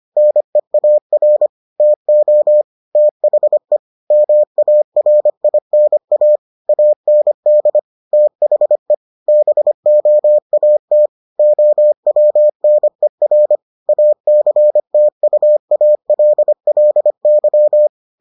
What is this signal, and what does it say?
Morse code, heard as a keyed tone, about 25 wpm, 600 Hz. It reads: NEAR TO THE MARINA AND THE BOAT OWNER ACTUALLY